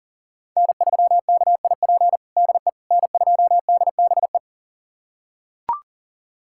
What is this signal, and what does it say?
Morse code, heard as a keyed tone, about 40 wpm, 700 Hz. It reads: N3KIP DE N2DBE
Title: USA Callsign DE Callsign - 5 Characters - 40wpm